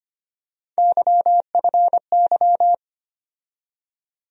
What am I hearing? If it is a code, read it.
Morse code, heard as a keyed tone, about 25 wpm, 700 Hz. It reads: YFY
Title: USA Call Sign Postfixes - 3 Letters 25wpm